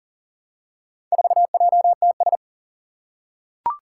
Morse code, heard as a keyed tone, about 40 wpm, 700 Hz. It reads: 4JTS
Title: Four Challenging Letter-Number Combinations 40wpm